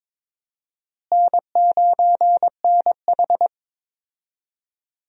Morse code, heard as a keyed tone, about 22 wpm, 700 Hz. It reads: N9NH
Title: USA Call Signs - 4 Characters - Repeated 3x - 22wpm